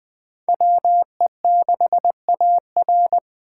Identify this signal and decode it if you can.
Morse code, heard as a keyed tone, about 21 wpm, 700 Hz. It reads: WE6AR
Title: Speed Racing - USA 2x2 Call Signs 17wpm